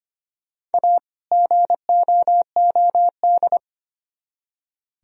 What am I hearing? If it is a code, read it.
Morse code, heard as a keyed tone, about 32 wpm, 700 Hz. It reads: A GOOD